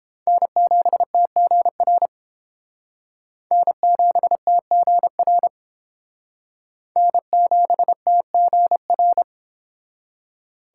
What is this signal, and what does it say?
Morse code, heard as a keyed tone, about 30 wpm, 700 Hz. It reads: N7TGR N7TGR N7TGR